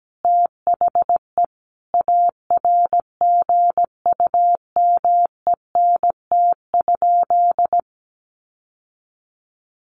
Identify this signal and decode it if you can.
Morse code, heard as a keyed tone, about 17 wpm, 700 Hz. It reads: THE ARGUMENT?
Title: Sentences from Top 1400 Words 17wpm